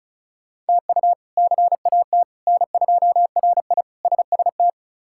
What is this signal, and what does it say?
Morse code, heard as a keyed tone, about 36 wpm, 700 Hz. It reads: TU CAT N2RI SST
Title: K1USN SST QSO Practice - 35wpm